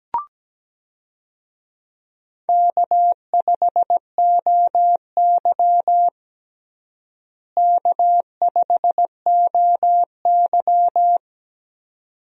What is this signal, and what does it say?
Morse code, heard as a keyed tone, about 17 wpm, 700 Hz. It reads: K5OY K5OY
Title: USA Call Signs - 4 Characters - Repeated 2x - 17wpm